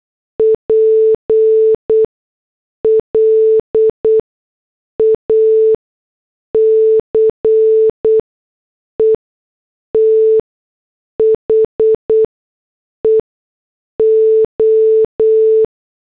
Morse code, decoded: PLACE THE O